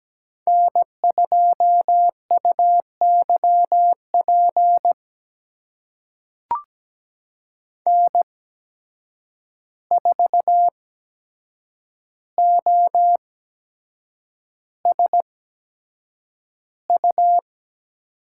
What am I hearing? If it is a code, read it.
Morse code, heard as a keyed tone, about 17 wpm, spaced as for 7 wpm, 700 Hz. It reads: N2UYP N4OSU
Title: USA Call Signs - 5 Characters - 8x Character Spacing - 17wpm